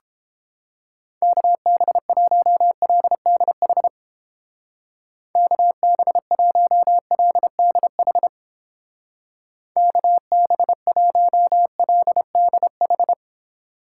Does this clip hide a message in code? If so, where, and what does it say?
Yes: Morse code from 1.2 to 13.1 s, heard as a keyed tone, about 30 wpm, 700 Hz. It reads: KB1LDH KB1LDH KB1LDH